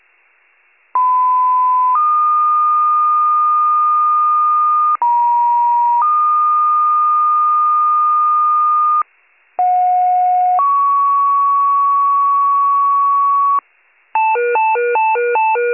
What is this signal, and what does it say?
Two-tone page, heard as a one-second tone, then a three-second tone, about 1010 then 1220 Hz, about 950 then 1220 Hz, about 730 then 1120 Hz. It is heard from 0.9 to 13.6 s.